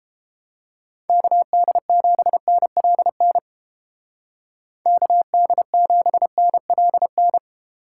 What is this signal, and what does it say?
Morse code, heard as a keyed tone, about 30 wpm, 700 Hz. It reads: KD7NLN KD7NLN